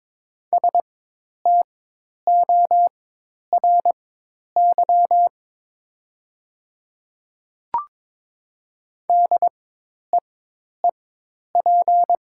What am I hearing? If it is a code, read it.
Morse code, heard as a keyed tone, about 23 wpm, spaced as for 10 wpm, 700 Hz. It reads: STORY DEEP
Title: 4x Character Spacing - Top 500 Words - 22wpm